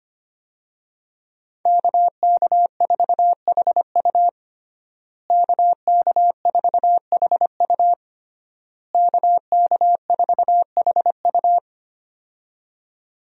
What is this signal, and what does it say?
Morse code, heard as a keyed tone, about 25 wpm, 700 Hz. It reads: KK4HU KK4HU KK4HU
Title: USA Call Signs - 5 Characters - Repeated 3x - 25wpm